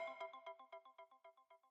There is no code.